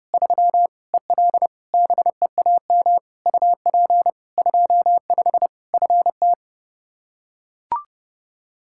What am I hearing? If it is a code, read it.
Morse code, heard as a keyed tone, about 30 wpm, 700 Hz. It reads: 3 EL BEAM UP 25 FT
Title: Practice QSO - Exchange 2 - Info - 30wpm